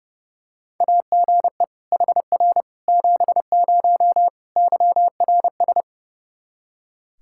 Morse code, heard as a keyed tone, about 32 wpm, 700 Hz. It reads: AGE HR 70 YRS
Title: Practice QSO - Exchange 3 - More Info - 30wpm